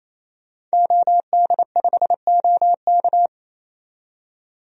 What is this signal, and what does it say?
Morse code, heard as a keyed tone, about 28 wpm, 700 Hz. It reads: OD5OK
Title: DXCC-100 Call Signs - 5 Characters - 28wpm